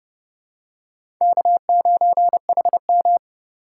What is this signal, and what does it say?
Morse code, heard as a keyed tone, about 30 wpm, 700 Hz. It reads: K9HM
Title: USA Call Signs - 4 Characters - 2x Character Spacing - 30wpm